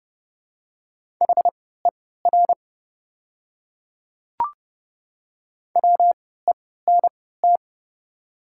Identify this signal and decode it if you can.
Morse code, heard as a keyed tone, about 30 wpm, spaced as for 17 wpm, 700 Hz. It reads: HER WENT
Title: 3x Character Spacing - Top 300 Words - 30wpm